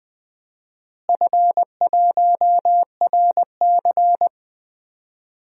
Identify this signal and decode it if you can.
Morse code, heard as a keyed tone, about 20 wpm, 700 Hz. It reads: F1RC
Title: DXCC-100 Call Signs - 4 Characters - Repeated 3x - 20wpm